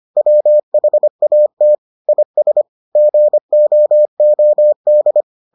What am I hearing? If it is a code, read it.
Morse code, heard as a keyed tone, about 25 wpm, 600 Hz. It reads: WHAT IS GOOD